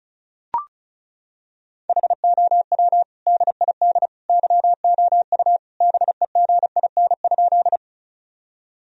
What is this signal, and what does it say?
Morse code, heard as a keyed tone, about 34 wpm, 700 Hz. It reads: HOW DID YOU BEGIN?
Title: Sentences from Top 1000 Words - Review Part 1 - 35wpm